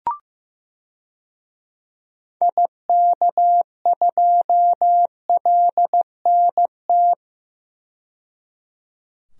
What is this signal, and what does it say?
Morse code, heard as a keyed tone, about 15 wpm, 700 Hz. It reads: IK2LNT
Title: DXCC-100 Call Signs - 6 Characters - 15wpm